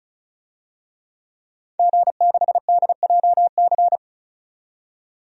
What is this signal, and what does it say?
Morse code, heard as a keyed tone, about 34 wpm, 700 Hz. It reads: G6DJC
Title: DXCC-100 Call Signs - 5 Characters - 35wpm